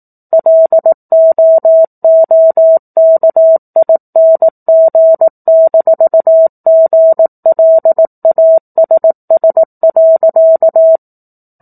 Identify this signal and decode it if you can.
Morse code, heard as a keyed tone, about 18 wpm, 650 Hz. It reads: LOOKING-GLASS.
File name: 18 WPM